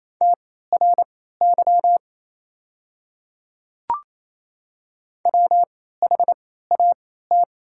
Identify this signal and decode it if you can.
Morse code, heard as a keyed tone, about 28 wpm, spaced as for 16 wpm, 690 Hz. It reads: TRY WHAT
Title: 3x Character Spacing - Top 200 Words - 28wpm